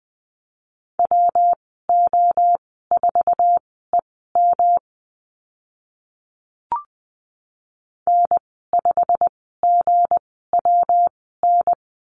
Morse code, heard as a keyed tone, about 20 wpm, spaced as for 14 wpm, 700 Hz. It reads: WO4EM N5GWN
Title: 173 Lesson 28g - Number 9 - Call Signs 20wpm